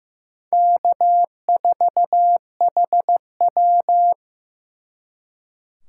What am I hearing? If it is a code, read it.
Morse code, heard as a keyed tone, about 18 wpm, 700 Hz. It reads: K4HW
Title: Speed Racing - USA 1x2 Call Signs 15wpm